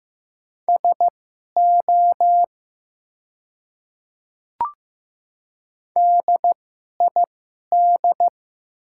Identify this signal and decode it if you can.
Morse code, heard as a keyed tone, about 15 wpm, spaced as for 11 wpm, 700 Hz. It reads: SO DID
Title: 2x Character Spacing - Top 300 Words - 15wpm